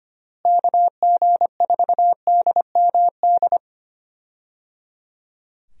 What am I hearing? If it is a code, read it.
Morse code, heard as a keyed tone, about 25 wpm, 700 Hz. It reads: KG4DMD
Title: USA Call Signs - 6 Characters - Repeated 3x - 25wpm